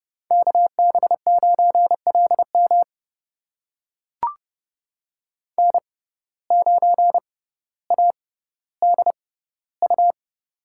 Morse code, heard as a keyed tone, about 30 wpm, spaced as for 15 wpm, 700 Hz. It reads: KB9LM N9ADU
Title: USA Call Signs - 5 Characters - 6x Character Spacing - 30wpm